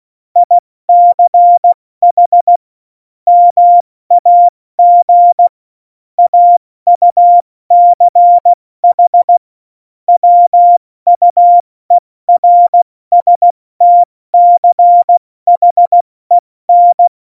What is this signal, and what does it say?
Morse code, heard as a keyed tone, about 16 wpm, 700 Hz. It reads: ICH MAG AUCH WUERSTCHEN
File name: Morsekurs / Witze / Witz 06 March 2018, 70 F80